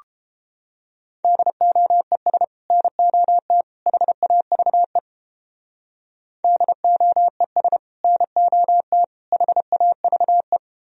Morse code, heard as a keyed tone, about 29 wpm, 700 Hz. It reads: DOES NOT HAVE DOES NOT HAVE